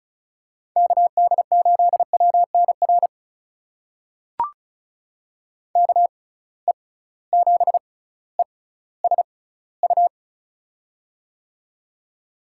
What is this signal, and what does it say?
Morse code, heard as a keyed tone, about 36 wpm, spaced as for 18 wpm, 700 Hz. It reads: KD8WNR KE7ESU
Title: USA Call Signs - 6 Characters - 6x Character Spacing - 35wpm